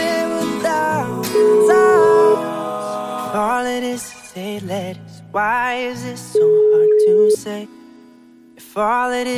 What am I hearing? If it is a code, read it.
Ringback tone (Europe, 425 Hz), with speech.